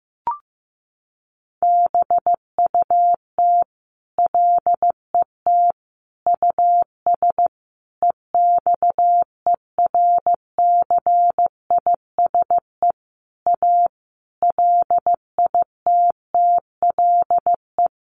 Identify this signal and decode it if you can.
Morse code, heard as a keyed tone, about 15 wpm, 700 Hz. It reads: BUT LET US EXERCISE A LITTLE
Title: Sentences from Top 1300 Words 15wpm